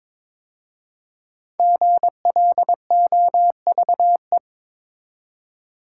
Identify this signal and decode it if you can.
Morse code, heard as a keyed tone, about 22 wpm, 700 Hz. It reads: GLOVE
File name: 5 Letter Words - First 3 Letters Given - 22wpm